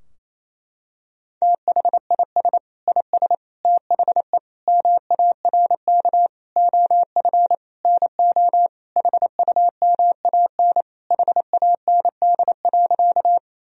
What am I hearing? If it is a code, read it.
Morse code, heard as a keyed tone, about 27 wpm, 700 Hz. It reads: THIS IS THE MARK OF NO HUMAN HAND.
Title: Sentences from Top 1000 Words - Review Part 3 - 28wpm